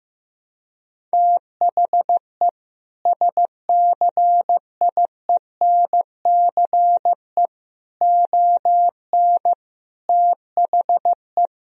Morse code, heard as a keyed tone, about 15 wpm, 700 Hz. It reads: THE SCIENCE ON THE
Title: Sentences from Top 1000 Words - Review Part 2 - 15wpm